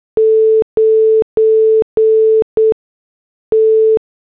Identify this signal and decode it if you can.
Morse code, heard as a keyed tone, about 8 wpm, 440 Hz. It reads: 9 T